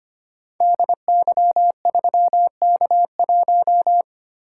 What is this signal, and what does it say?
Morse code, heard as a keyed tone, about 25 wpm, 700 Hz. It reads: DY3K1
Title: Code Groups - 5 Letter:Numbers - 25wpm